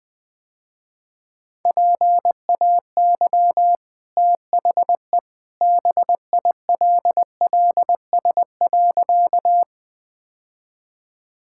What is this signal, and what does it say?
Morse code, heard as a keyed tone, about 21 wpm, 700 Hz. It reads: PAY THE BILLS.
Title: Easy Sentences with Hints - 20wpm